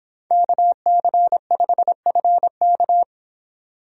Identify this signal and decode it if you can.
Morse code, heard as a keyed tone, about 26 wpm, 700 Hz. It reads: KC5FK